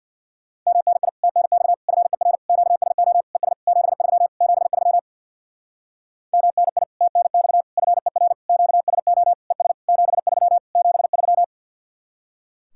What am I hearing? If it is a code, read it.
Morse code, heard as a keyed tone, about 62 wpm, 700 Hz. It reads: MNI TNX FER QSO ES 73 73 MNI TNX FER QSO ES 73 73
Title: Speed-Racing Common QSO Phrases Encore - 50wpm